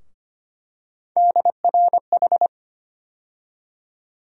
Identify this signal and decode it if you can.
Morse code, heard as a keyed tone, about 25 wpm, 700 Hz. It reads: DRH
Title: USA Call Sign Postfixes - 3 Letters 25wpm